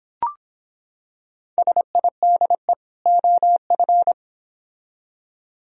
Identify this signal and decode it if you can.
Morse code, heard as a keyed tone, about 26 wpm, 700 Hz. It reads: SIDE OF